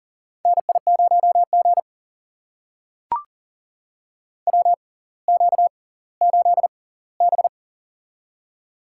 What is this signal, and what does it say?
Morse code, heard as a keyed tone, about 38 wpm, spaced as for 20 wpm, 700 Hz. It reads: NI0G WQ8B
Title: USA Call Signs - 4 Characters - 6x Character Spacing - 40wpm